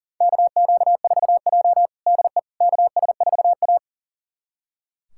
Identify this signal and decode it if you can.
Morse code, heard as a keyed tone, about 40 wpm, 700 Hz. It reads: KQ4J DE KS4A
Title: USA Callsign DE Callsign - 4 Characters - 40wpm